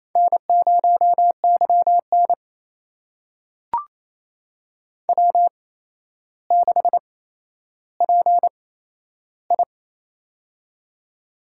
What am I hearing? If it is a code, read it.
Morse code, heard as a keyed tone, about 27 wpm, spaced as for 12 wpm, 700 Hz. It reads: N0YN W6PI
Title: USA Call Signs - 4 Characters - 8x Character Spacing - 28wpm